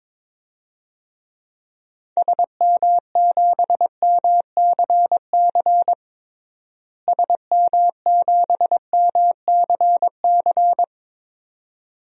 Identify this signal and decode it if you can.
Morse code, heard as a keyed tone, about 22 wpm, 700 Hz. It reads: SM7MCC SM7MCC
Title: DXCC-100 Call Signs - 6 Characters - Repeated 2x - 22wpm